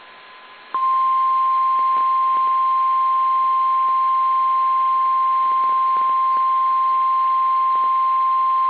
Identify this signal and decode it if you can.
Emergency Alert System (SAME) tones (the 1050 Hz attention tone). It is heard from 1.0 to 8.3 s.
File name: eas / KSDZKDJL / KSDZKDJL